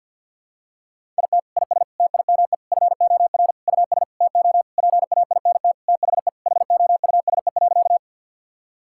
Morse code, heard as a keyed tone, about 50 wpm, 700 Hz. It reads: IT IS TIME FOR US TO PAINT THE HOUSE.
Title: Sentences from Top 1000 Words - Review Part 6 - 50wpm